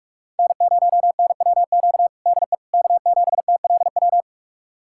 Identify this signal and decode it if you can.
Morse code, heard as a keyed tone, about 46 wpm, 700 Hz. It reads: N0NWQ DE K7TLW
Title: USA Callsign DE Callsign - 5 Characters - 45wpm